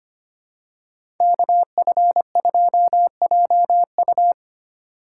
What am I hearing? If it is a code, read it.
Morse code, heard as a keyed tone, about 25 wpm, 700 Hz. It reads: KF2JU
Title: USA Call Signs - 3x Character Spacing - 25wpm